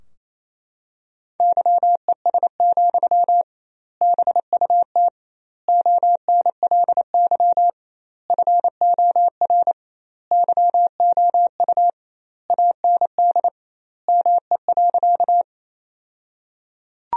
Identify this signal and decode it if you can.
Morse code, heard as a keyed tone, about 28 wpm, 700 Hz. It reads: YES, BUT ONLY FOR YOU AND ME.
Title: 2x Word-Spacing - Sentences from Top 500 Words - 28wpm